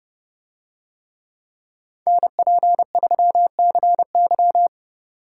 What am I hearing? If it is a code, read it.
Morse code, heard as a keyed tone, about 30 wpm, 700 Hz. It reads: NP3CY